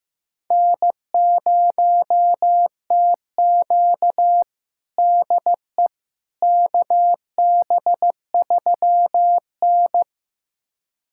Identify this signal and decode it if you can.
Morse code, heard as a keyed tone, about 15 wpm, 700 Hz. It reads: N0TQ DE KB3N
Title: USA Callsign DE Callsign - 4 Characters - 15wpm